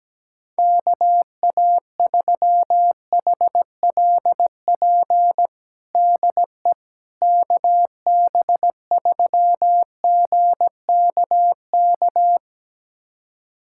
Morse code, heard as a keyed tone, about 17 wpm, 700 Hz. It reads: KA3HLP DE KB3GKK
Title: USA Callsign DE Callsign 17wpm